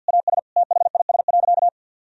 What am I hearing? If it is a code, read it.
Morse code, heard as a keyed tone, about 49 wpm, 700 Hz. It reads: AS THIS.
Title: Sentences from Top 1000 Words - Review Part 1 - 50wpm